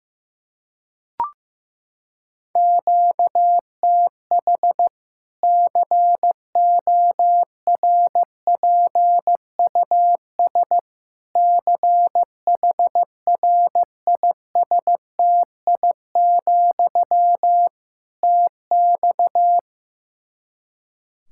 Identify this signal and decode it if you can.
Morse code, heard as a keyed tone, about 15 wpm, 700 Hz. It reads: QTH CORPUS CHRISTI, TX
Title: K1USN SST QSO Practice 15wpm